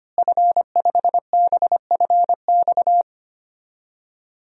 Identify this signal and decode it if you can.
Morse code, heard as a keyed tone, about 25 wpm, 700 Hz. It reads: F5BFX
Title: DXCC-100 Call Signs - 5 Characters - Repeated 3x - 25wpm